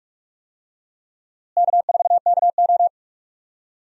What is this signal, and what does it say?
Morse code, heard as a keyed tone, about 46 wpm, 700 Hz. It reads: K4KO
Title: USA Call Signs - 4 Characters - Repeated 2x - 45wpm